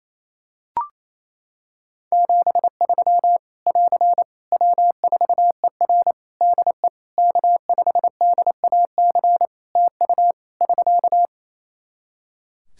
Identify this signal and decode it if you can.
Morse code, heard as a keyed tone, about 27 wpm, 700 Hz. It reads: 73 <AR> W4ER DE K5DAC TU <SK>
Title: K1USN SST QSO Practice 28wpm